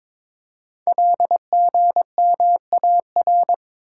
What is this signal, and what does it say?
Morse code, heard as a keyed tone, about 22 wpm, 700 Hz. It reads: LGMAR
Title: Code Groups - 5 Letters - 22wpm